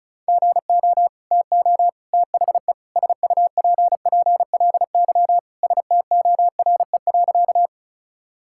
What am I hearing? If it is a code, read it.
Morse code, heard as a keyed tone, about 36 wpm, 700 Hz. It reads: GO TO THE SUPPLY STORE.
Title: Sentences from Top 1000 Words - Review Part 5 - 35wpm